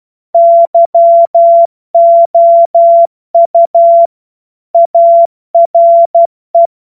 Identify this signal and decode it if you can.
Morse code, heard as a keyed tone, about 12 wpm, 680 Hz. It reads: YOU ARE
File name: Morsekurs / Witze